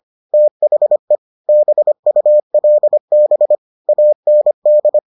Morse code, heard as a keyed tone, about 25 wpm, 600 Hz. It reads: THE BULB AND